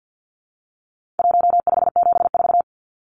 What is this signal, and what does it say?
Morse code, heard as a keyed tone, about 50 wpm, 700 Hz. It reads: 1574
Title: Numbers - 4-Digit 50wpm